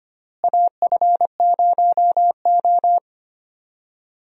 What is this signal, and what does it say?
Morse code, heard as a keyed tone, about 25 wpm, 700 Hz. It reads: AF0O
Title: USA Call Signs - 4 Characters - Repeated 3x - 25wpm